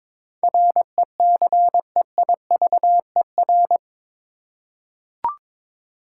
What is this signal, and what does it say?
Morse code, heard as a keyed tone, about 21 wpm, 700 Hz. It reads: RECEIVER
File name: practice / 22wpm / Eight Letter Words - 22wpm